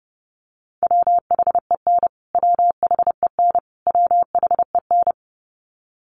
Morse code, heard as a keyed tone, about 30 wpm, 700 Hz. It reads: WHEN WHEN WHEN
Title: Top 500 Words Repeated 30wpm